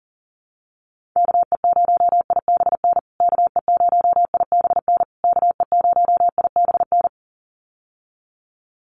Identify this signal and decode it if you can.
Morse code, heard as a keyed tone, about 40 wpm, 700 Hz. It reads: KE0IBN KE0IBN KE0IBN
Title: USA Call Signs Repeated 3x 40wpm